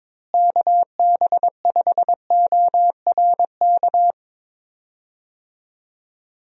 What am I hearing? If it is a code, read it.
Morse code, heard as a keyed tone, about 22 wpm, 700 Hz. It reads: KB5ORK
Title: USA Call Signs - 6 Characters - Repeated 2x - 22wpm